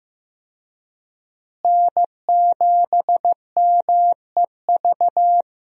Morse code, heard as a keyed tone, about 15 wpm, 700 Hz. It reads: N7MEV
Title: USA Call Signs - 5 Characters - Repeated 2x - 15wpm